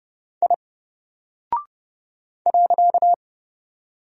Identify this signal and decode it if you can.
Morse code, heard as a keyed tone, about 30 wpm, 700 Hz. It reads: I .
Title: Single Character (Letter:Number:Punctuation:Prosign) Alt 30wpm